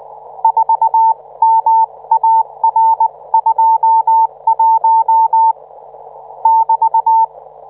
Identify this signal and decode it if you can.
Morse code, heard as a keyed tone, about 19 wpm, 900 Hz. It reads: 4MAR21 <BT>